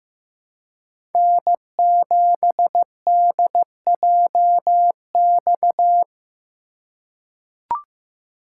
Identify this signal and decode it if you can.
Morse code, heard as a keyed tone, about 15 wpm, 700 Hz. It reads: N7DJX
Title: USA Call Signs - 5 Characters - 4x Character Spacing - 15wpm